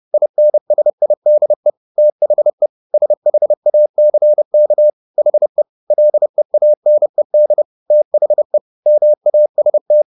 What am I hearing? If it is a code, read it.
Morse code, heard as a keyed tone, about 30 wpm, 600 Hz. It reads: INSIDE THE SHACK HE LEANED THE MAST